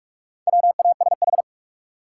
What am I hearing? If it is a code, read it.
Morse code, heard as a keyed tone, about 45 wpm, 700 Hz. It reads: WASH
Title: Four Letter Words - 45wpm